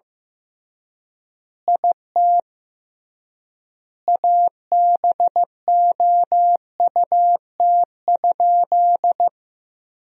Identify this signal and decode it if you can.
Morse code, heard as a keyed tone, about 15 wpm, 700 Hz. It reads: IT ABOUT?
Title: 3x Word-Spacing - Sentences from Top 100 Words - 15wpm